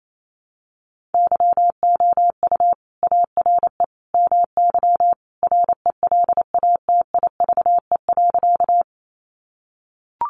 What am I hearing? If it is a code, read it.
Morse code, heard as a keyed tone, about 28 wpm, 700 Hz. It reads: YOU ARE MY RELATIVE.
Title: Sentences from Top 1400 Words 28wpm